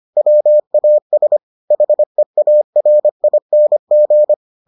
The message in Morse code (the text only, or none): WAS HEARING